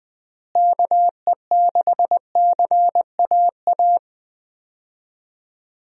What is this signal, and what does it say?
Morse code, heard as a keyed tone, about 20 wpm, 700 Hz. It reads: KE6CAA
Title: USA Call Signs - 6 Characters - Repeated 3x - 20wpm